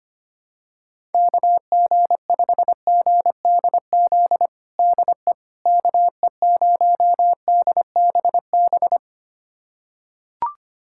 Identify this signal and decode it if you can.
Morse code, heard as a keyed tone, about 25 wpm, 700 Hz. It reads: KG5GDZ DE KE0DBB
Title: USA Callsign DE Callsign - 6 Characters - 25wpm